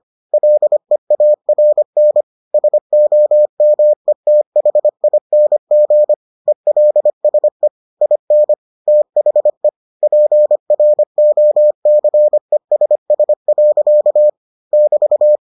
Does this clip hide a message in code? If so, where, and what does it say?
Yes: Morse code from 0.3 to 15.4 s, heard as a keyed tone, about 25 wpm, 600 Hz. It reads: LEARN SOMETHING ELSE IN THE PROCESS. <BT>